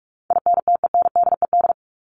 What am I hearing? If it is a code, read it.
Morse code, heard as a keyed tone, about 44 wpm, 700 Hz. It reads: INTENDED
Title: ED Suffix - Top 100 Words with Base Words 45wpm